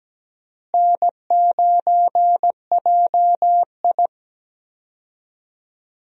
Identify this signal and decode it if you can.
Morse code, heard as a keyed tone, about 17 wpm, 700 Hz. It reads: N9JI
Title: USA Call Signs - 4 Characters - Repeated 3x - 17wpm